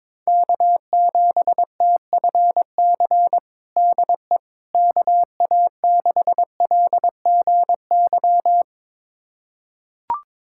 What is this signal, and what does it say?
Morse code, heard as a keyed tone, about 22 wpm, 700 Hz. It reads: K7TFC DE KA6LGY
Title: USA Callsign DE Callsign 22wpm